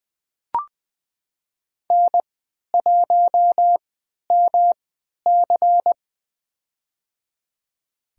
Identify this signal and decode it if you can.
Morse code, heard as a keyed tone, about 20 wpm, spaced as for 11 wpm, 700 Hz. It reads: N1MC
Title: USA Call Signs - 3x Character Spacing - 20wpm